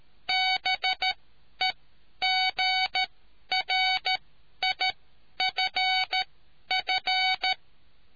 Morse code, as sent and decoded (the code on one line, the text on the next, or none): -... --. .-. .. ..-. ..-.
BEGRIFF